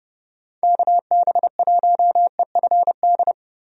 Morse code, heard as a keyed tone, about 32 wpm, 700 Hz. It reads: KB1EFD
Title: USA Call Signs - 6 Characters - Repeated 2x - 30wpm